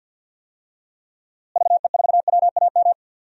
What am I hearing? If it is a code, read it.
Morse code, heard as a keyed tone, about 52 wpm, 700 Hz. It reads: VE4WAM
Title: DXCC-100 Call Signs - 6 Characters - Repeated 3x - 50wpm